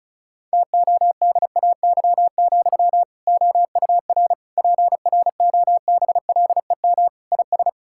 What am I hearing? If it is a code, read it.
Morse code, heard as a keyed tone, about 36 wpm, 700 Hz. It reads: TODAY, OUR PROBLEM IS
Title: Sentences from Top 1600 Words 35wpm